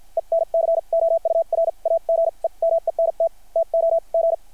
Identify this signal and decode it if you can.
Morse code, heard as a keyed tone, about 50 wpm, spaced as for 40 wpm, 660 Hz. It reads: ENCOURAGEMENT TOG
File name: archived / CRSnet